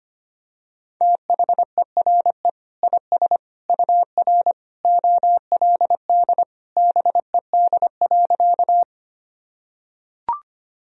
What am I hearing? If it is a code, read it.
Morse code, heard as a keyed tone, about 25 wpm, 700 Hz. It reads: THERE IS UR OLD BED.
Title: Sentences from Top 500 Words and Abbreviations 25wpm